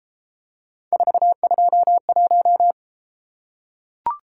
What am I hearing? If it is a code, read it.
Morse code, heard as a keyed tone, about 32 wpm, 700 Hz. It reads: 421